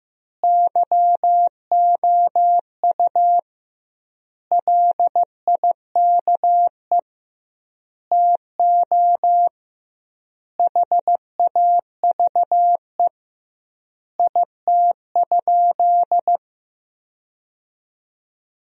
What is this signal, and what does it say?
Morse code, heard as a keyed tone, about 16 wpm, 700 Hz. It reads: YOU LIKE TO HAVE IT?
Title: 2x Word-Spacing - Sentences from Top 100 Words - 15wpm